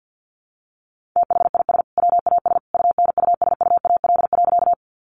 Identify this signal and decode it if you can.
Morse code, heard as a keyed tone, about 50 wpm, 700 Hz. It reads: THIS WAS UNUSUAL.
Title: Sentences from Top 1400 Words 50wpm